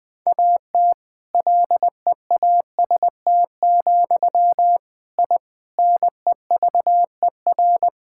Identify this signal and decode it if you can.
Morse code, heard as a keyed tone, about 21 wpm, 700 Hz. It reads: AT LEAST, I NEVER
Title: Sentences from Top 1000 Words - Review Part 5 - 20wpm